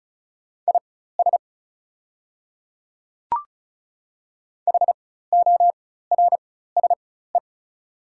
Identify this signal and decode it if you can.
Morse code, heard as a keyed tone, about 36 wpm, spaced as for 16 wpm, 690 Hz. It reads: IS HORSE
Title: 4x Character Spacing - Top 300 Words - 35wpm